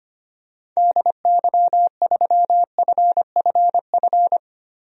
Morse code, heard as a keyed tone, about 25 wpm, 700 Hz. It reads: DY3FFF
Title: DXCC-100 Call Signs - 6 Characters - 25wpm